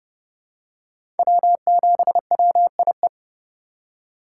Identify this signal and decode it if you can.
Morse code, heard as a keyed tone, about 30 wpm, 700 Hz. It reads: W7WIE
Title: USA Call Signs - 5 Characters - 3x Character Spacing - 30wpm